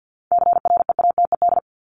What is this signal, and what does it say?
Morse code, heard as a keyed tone, about 50 wpm, 700 Hz. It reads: CREATED
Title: ED Suffix - Top 100 Words with Base Words 50wpm